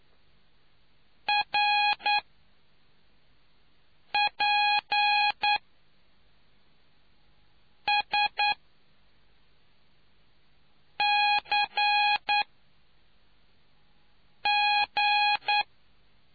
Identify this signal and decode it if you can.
Morse code, heard as a keyed tone, about 9 wpm, spaced as for 3 wpm, 830 Hz. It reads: RPSCG